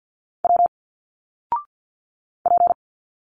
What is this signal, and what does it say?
Morse code, heard as a keyed tone, about 51 wpm, 700 Hz. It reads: W P
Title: Single Letters 50wpm